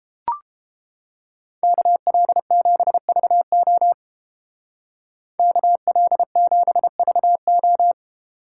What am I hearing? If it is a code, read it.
Morse code, heard as a keyed tone, about 31 wpm, 700 Hz. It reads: KL7VO KL7VO